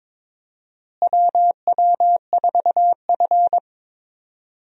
Morse code, heard as a keyed tone, about 22 wpm, 700 Hz. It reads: WW4F
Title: USA Call Signs - 4 Characters - Repeated 3x - 22wpm